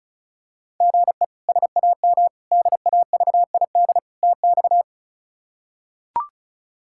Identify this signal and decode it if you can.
Morse code, heard as a keyed tone, about 34 wpm, 700 Hz. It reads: GE SAM DAVID TX
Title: K1USN SST QSO Practice - 35wpm